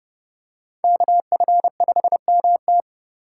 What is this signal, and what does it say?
Morse code, heard as a keyed tone, about 30 wpm, 700 Hz. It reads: KF5MT
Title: USA Call Signs - 5 Characters - 2x Character Spacing - 30wpm